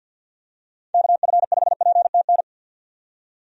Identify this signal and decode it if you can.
Morse code, heard as a keyed tone, about 52 wpm, 700 Hz. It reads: KF5PTN
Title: USA Call Signs - 6 Characters - 50wpm